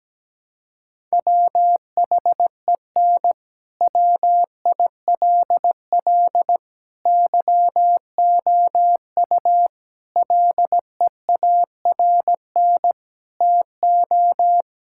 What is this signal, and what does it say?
Morse code, heard as a keyed tone, about 17 wpm, 700 Hz. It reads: WHEN WILL YOU LEARN TO
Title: Sentences from Top 1000 Words - Review Part 6 - 17wpm